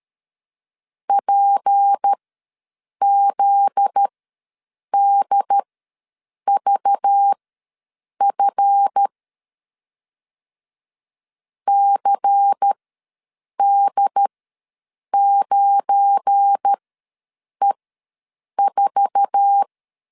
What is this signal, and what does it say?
Morse code, heard as a keyed tone, about 13 wpm, spaced as for 7 wpm, 800 Hz. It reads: PZDVF CD9E4
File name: LEZIONI / 8 YJ45 / 8 YJ4 ws 12wpm 3249